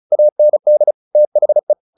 Morse code, heard as a keyed tone, about 34 wpm, 600 Hz. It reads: AND THE
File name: the old man and the sea 2 35wpm